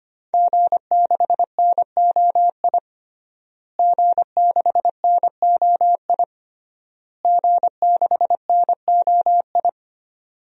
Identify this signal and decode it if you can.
Morse code, heard as a keyed tone, about 25 wpm, 700 Hz. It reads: G6NOI G6NOI G6NOI
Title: DXCC-100 Call Signs - 5 Characters - Repeated 3x - 25wpm